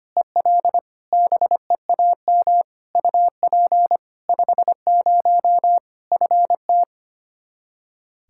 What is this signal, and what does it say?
Morse code, heard as a keyed tone, about 25 wpm, 700 Hz. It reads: EL BEAM UP 50 FT
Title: Practice QSO - Exchange 2 - Info - 25wpm